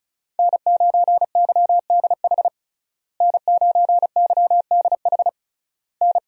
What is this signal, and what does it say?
Morse code, heard as a keyed tone, about 36 wpm, 700 Hz. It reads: N9YDH N9YDH N
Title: USA Call Signs - 5 Characters - Repeated 3x - 35wpm